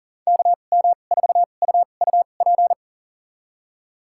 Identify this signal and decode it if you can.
Morse code, heard as a keyed tone, about 40 wpm, spaced as for 29 wpm, 700 Hz. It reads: KM4UUP
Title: USA Call Signs - 2x Character Spacing - 40wpm